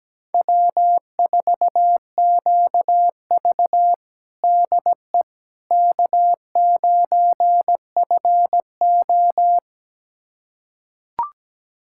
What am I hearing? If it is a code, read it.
Morse code, heard as a keyed tone, about 17 wpm, 700 Hz. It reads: W4QV DE K9FO
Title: USA Callsign DE Callsign - 4 Characters - 17wpm